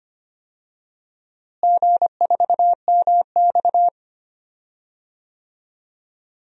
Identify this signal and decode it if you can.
Morse code, heard as a keyed tone, about 25 wpm, 700 Hz. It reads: G4MX
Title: DXCC-100 Call Signs - 4 Characters - Repeated 3x - 25wpm